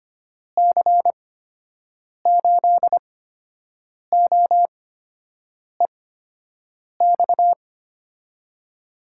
Morse code, heard as a keyed tone, about 25 wpm, spaced as for 7 wpm, 700 Hz. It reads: C8OEX
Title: USA Call Signs - 6 Characters - 8x Character Spacing - 25wpm